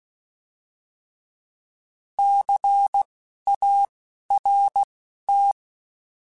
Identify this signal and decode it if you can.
Morse code, heard as a keyed tone, about 16 wpm, spaced as for 12 wpm, 800 Hz. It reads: CART
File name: Alphabetical / Alphabetical mp3s / Cs @10